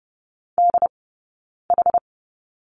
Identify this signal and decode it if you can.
Morse code, heard as a keyed tone, about 30 wpm, 700 Hz. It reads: D H
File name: Lesson 17d - Letter u - Two Characters 30wpm